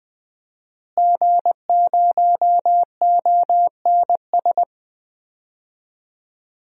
Morse code, heard as a keyed tone, about 21 wpm, 700 Hz. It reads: G0ONS
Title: DXCC-100 Call Signs - 5 Characters - Repeated 3x - 20wpm